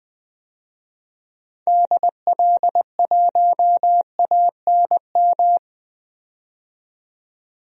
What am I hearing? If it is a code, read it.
Morse code, heard as a keyed tone, about 20 wpm, 700 Hz. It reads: DL1ANM
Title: DXCC-100 Call Signs - 6 Characters - 20wpm